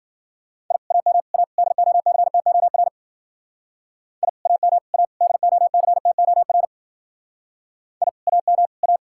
Morse code, heard as a keyed tone, about 47 wpm, 700 Hz. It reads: I AM A DOCTOR I AM A DOCTOR I AM A
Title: Speed-Racing Common QSO Phrases - 40wpm